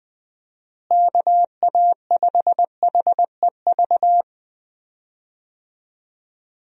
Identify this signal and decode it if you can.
Morse code, heard as a keyed tone, about 20 wpm, 700 Hz. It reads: KA5HEV
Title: USA Call Signs - 6 Characters - Repeated 3x - 20wpm